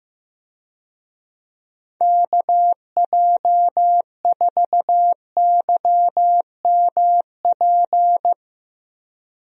Morse code, heard as a keyed tone, about 15 wpm, 700 Hz. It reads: KJ4YMP
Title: USA Call Signs - 6 Characters - Repeated 3x - 15wpm